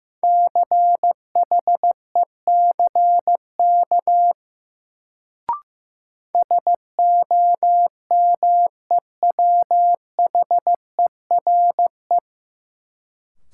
Morse code, heard as a keyed tone, about 15 wpm, 700 Hz. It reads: CHECK SOMEWHERE
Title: Sentences from Top 1400 Words - New Word Drill Alt - 15wpm